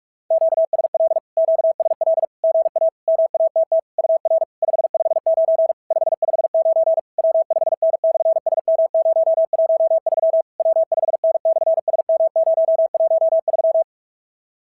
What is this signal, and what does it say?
Morse code, heard as a keyed tone, about 50 wpm, 650 Hz. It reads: QSL QSL GA MATT UR 559 559 W5N/SM013 W5N/SM013
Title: SOTA QSO Practice 45wpm